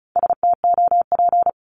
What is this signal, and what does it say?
Morse code, heard as a keyed tone, about 36 wpm, 700 Hz. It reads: STOP
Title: Multi-letter Phonemes 35wpm